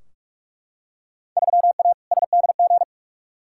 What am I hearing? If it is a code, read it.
Morse code, heard as a keyed tone, about 45 wpm, 700 Hz. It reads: 3A SDG
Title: Field Day Exchanges - 45wpm